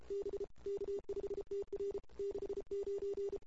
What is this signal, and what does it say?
Morse code, heard as a keyed tone, about 32 wpm, 410 Hz. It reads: B K5TR 69
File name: CW / 2008-ss-cw-k5tr-op-n5rz -23:00